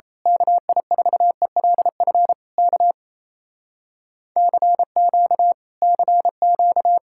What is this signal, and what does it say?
Morse code, heard as a keyed tone, about 26 wpm, 700 Hz. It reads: KI4ELF K CQ CQ
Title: Speed-Racing Common QSO Phrases - 28wpm